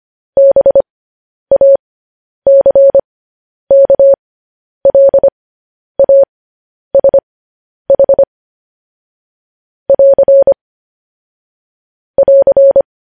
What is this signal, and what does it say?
Morse code, heard as a keyed tone, about 25 wpm, spaced as for 10 wpm, 550 Hz. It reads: BACKLASH <AR> <AR>
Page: news headlines in morse code at 10 wpm